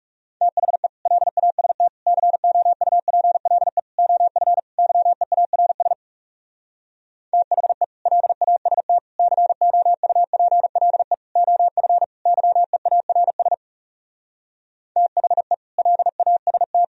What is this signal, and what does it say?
Morse code, heard as a keyed tone, about 40 wpm, 700 Hz. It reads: THE LAST COUPLE OF YEARS THE LAST COUPLE OF YEARS THE LAST